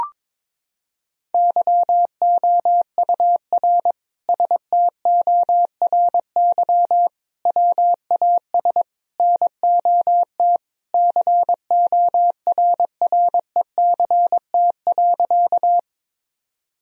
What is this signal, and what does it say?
Morse code, heard as a keyed tone, about 22 wpm, 700 Hz. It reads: YOUR STORY WAS NOT CORRECT.
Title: Sentences from Top 1000 Words - Review Part 2 - 22wpm